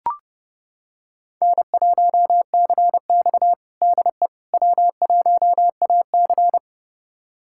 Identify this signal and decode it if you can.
Morse code, heard as a keyed tone, about 30 wpm, 700 Hz. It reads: N1CX DE W1AC
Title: USA Callsign DE Callsign - 4 Characters - 30wpm